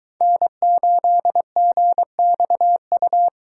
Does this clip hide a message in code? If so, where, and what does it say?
Yes: Morse code from 0.2 to 3.3 s, heard as a keyed tone, about 23 wpm, 700 Hz. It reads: N8GXU